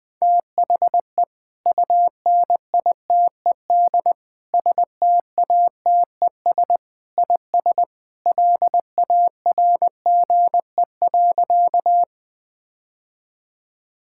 Morse code, heard as a keyed tone, about 20 wpm, 700 Hz. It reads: THE UNITED STATES IS LARGE.
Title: Sentences with 3-Word Phrases - Spoken Beginning - 20wpm